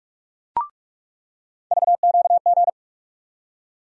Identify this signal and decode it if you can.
Morse code, heard as a keyed tone, about 46 wpm, 700 Hz. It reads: VQG
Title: USA Call Sign Postfixes - 3 Letters 45wpm